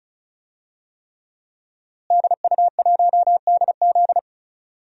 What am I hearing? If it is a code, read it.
Morse code, heard as a keyed tone, about 36 wpm, 700 Hz. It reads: DU1DZ
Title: DXCC-100 Call Signs - 5 Characters - Repeated 3x - 35wpm